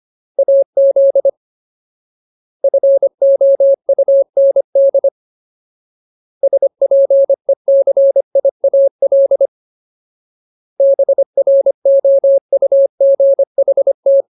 Morse code, heard as a keyed tone, about 25 wpm, 550 Hz. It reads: AZ FOUND SPECIAL BROUGHT
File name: Ultimate Word List / 25WPM Ultimate